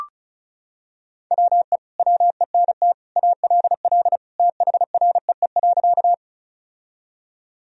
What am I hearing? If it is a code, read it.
Morse code, heard as a keyed tone, about 36 wpm, 700 Hz. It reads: WE WENT ALL THREE.
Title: Sentences from Top 1000 Words - Review Part 1 - 35wpm